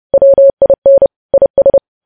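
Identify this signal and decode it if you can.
Morse code, heard as a keyed tone, about 30 wpm, 550 Hz. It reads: WIN IS
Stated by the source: News headlines sent in Morse code at 30 wpm.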